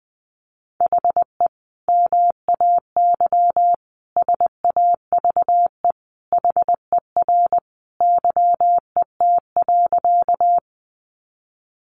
Morse code, heard as a keyed tone, about 20 wpm, 700 Hz. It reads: HE MAY SAVE HER YET.
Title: Sentences from Top 1100 Words 20wpm